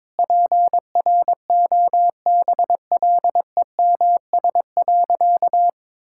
Morse code, heard as a keyed tone, about 22 wpm, 700 Hz. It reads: PROBLEMS.
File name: Difficult Sentences with Hints - 22wpm